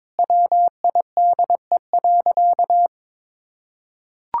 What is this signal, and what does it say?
Morse code, heard as a keyed tone, about 22 wpm, 700 Hz. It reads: WIDE.
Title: Sentences from Top 1000 Words - Review Part 6 - 22wpm